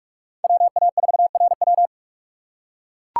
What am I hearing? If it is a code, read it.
Morse code, heard as a keyed tone, about 43 wpm, 700 Hz. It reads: WA4RW
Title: USA Call Signs - 5 Characters - 3x Character Spacing - 45wpm